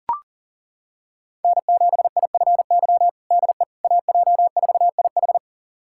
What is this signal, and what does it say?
Morse code, heard as a keyed tone, about 40 wpm, 700 Hz. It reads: N7IFY DE AJ4IH
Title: USA Callsign DE Callsign - 5 Characters - 40wpm